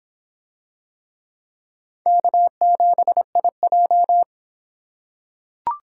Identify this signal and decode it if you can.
Morse code, heard as a keyed tone, about 26 wpm, 700 Hz. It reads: K7IJ